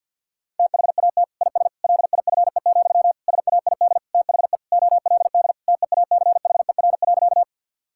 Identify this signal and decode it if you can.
Morse code, heard as a keyed tone, about 50 wpm, 700 Hz. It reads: THAT IS LIFE, SAID THE OLD TEACHER.
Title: Sentences from Top 1000 Words - Review Part 6 - 50wpm